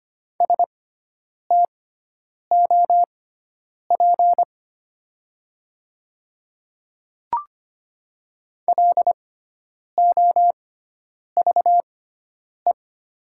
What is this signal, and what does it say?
Morse code, heard as a keyed tone, about 25 wpm, spaced as for 9 wpm, 700 Hz. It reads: STOP LOVE
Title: 6x Character Spacing - Top 500 Words - 25wpm